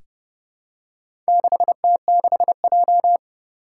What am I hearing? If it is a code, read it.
Morse code, heard as a keyed tone, about 30 wpm, 690 Hz. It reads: 6T6J
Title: Four Challenging Letter-Number Combinations 30wpm